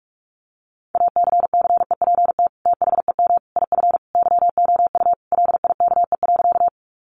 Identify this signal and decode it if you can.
Morse code, heard as a keyed tone, about 44 wpm, 700 Hz. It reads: ACCEPT THEM IF YOU LIKE.
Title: Sentences from Top 1300 Words 45wpm